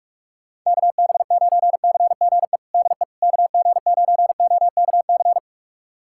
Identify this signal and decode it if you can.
Morse code, heard as a keyed tone, about 46 wpm, 700 Hz. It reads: KB9CGE DE KG9OKC
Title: USA Callsign DE Callsign - 6 Characters - 45wpm